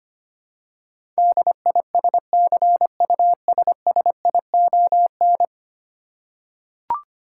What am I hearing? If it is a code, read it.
Morse code, heard as a keyed tone, about 25 wpm, 700 Hz. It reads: DISCUSSION
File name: 10 Letter Words - First 3 Letters Given - 25wpm